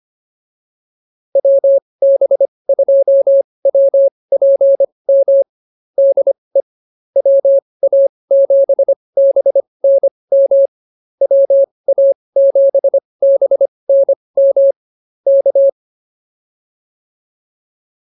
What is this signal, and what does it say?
Morse code, heard as a keyed tone, about 24 wpm, spaced as for 20 wpm, 550 Hz. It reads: WB2WPM DE WA7BNM WA7BNM K
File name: QSO106 20